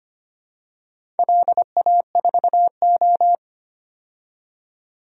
Morse code, heard as a keyed tone, about 25 wpm, 700 Hz. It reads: LA4O
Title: DXCC-100 Call Signs - 4 Characters - 25wpm